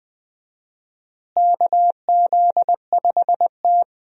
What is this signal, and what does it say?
Morse code, heard as a keyed tone, about 20 wpm, 700 Hz. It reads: KZ5T
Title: USA Call Signs - 4 Characters - Repeated 2x - 20wpm